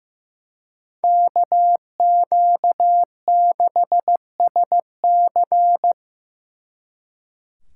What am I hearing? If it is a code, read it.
Morse code, heard as a keyed tone, about 17 wpm, 700 Hz. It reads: KQ6SC
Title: Speed Racing - USA 2x2 Call Signs 15wpm